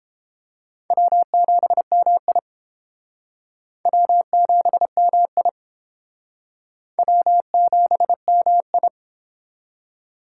Morse code, heard as a keyed tone, about 29 wpm, 700 Hz. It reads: W7MI W7MI W7MI